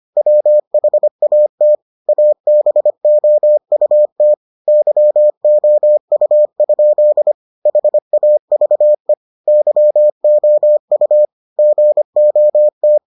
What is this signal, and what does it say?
Morse code, heard as a keyed tone, about 25 wpm, 600 Hz. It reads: WHAT ABOUT YOU? HAVE YOU GOT